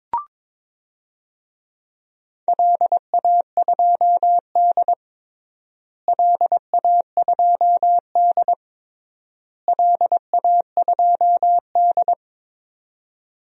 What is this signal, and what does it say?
Morse code, heard as a keyed tone, about 22 wpm, 700 Hz. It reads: LA2D LA2D LA2D
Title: DXCC-100 Call Signs - 4 Characters - Repeated 3x - 22wpm